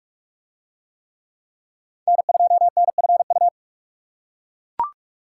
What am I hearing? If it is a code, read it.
Morse code, heard as a keyed tone, about 40 wpm, 700 Hz. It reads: N2NFU